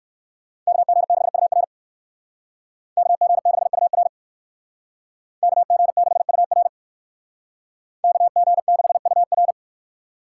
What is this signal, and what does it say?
Morse code, heard as a keyed tone, about 58 wpm, 700 Hz. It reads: KG6UR KG6UR KG6UR KG6UR